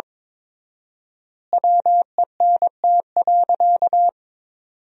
Morse code, heard as a keyed tone, about 23 wpm, 700 Hz. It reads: WENT.
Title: 4x Word-Spacing - Sentences from Top 200 Words - 22wpm